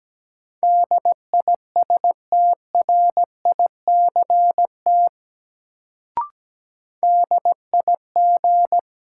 Morse code, heard as a keyed tone, about 17 wpm, 700 Hz. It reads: DISTRICT DIG
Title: Sentences from Top 1600 Words - New Word Drill - 17wpm